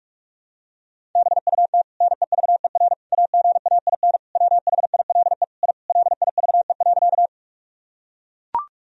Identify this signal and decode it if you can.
Morse code, heard as a keyed tone, about 44 wpm, 700 Hz. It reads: BUT NEVER AGAIN WHILE I LIVE.
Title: Sentences from Top 1000 Words - Review Part 1 - 45wpm